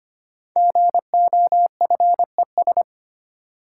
Morse code, heard as a keyed tone, about 25 wpm, 700 Hz. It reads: GOFES
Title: Code Groups - 5 Letters - 25wpm